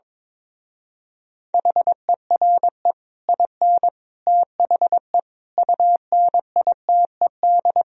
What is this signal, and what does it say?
Morse code, heard as a keyed tone, about 23 wpm, 700 Hz. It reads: HERE IN THE UNITED
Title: Speed-Racing - Sets of 5 Words 22wpm